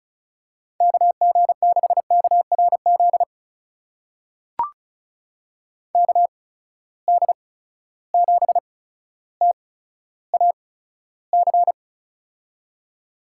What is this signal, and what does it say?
Morse code, heard as a keyed tone, about 36 wpm, spaced as for 15 wpm, 700 Hz. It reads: KG6KRZ KD7TAC
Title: USA Call Signs - 6 Characters - 8x Character Spacing - 35wpm